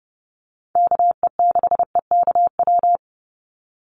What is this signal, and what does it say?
Morse code, heard as a keyed tone, about 30 wpm, 700 Hz. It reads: KE6EKW
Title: USA Call Signs 30wpm